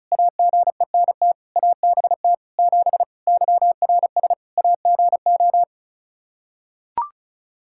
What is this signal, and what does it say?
Morse code, heard as a keyed tone, about 36 wpm, 700 Hz. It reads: AGENT ABT 7 YRS AGO
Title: K1USN SST QSO Practice 35wpm